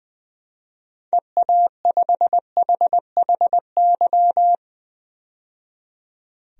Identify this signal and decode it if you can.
Morse code, heard as a keyed tone, about 21 wpm, 700 Hz. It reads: EA5HHY
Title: DXCC-100 Call Signs - 6 Characters - 20wpm